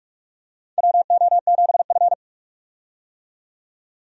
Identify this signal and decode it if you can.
Morse code, heard as a keyed tone, about 41 wpm, 700 Hz. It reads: WO7F
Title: USA Call Signs - 4 Characters - 45wpm